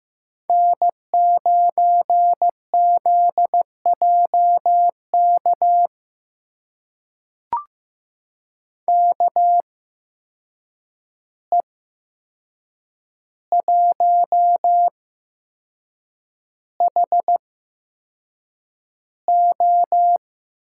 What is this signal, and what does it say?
Morse code, heard as a keyed tone, about 15 wpm, spaced as for 6 wpm, 700 Hz. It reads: N9ZJK KE1HO
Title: USA Call Signs - 5 Characters - 8x Character Spacing - 15wpm